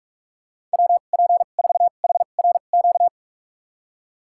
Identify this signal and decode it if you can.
Morse code, heard as a keyed tone, about 46 wpm, spaced as for 32 wpm, 700 Hz. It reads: WP4HRQ
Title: USA Call Signs - 2x Character Spacing - 45wpm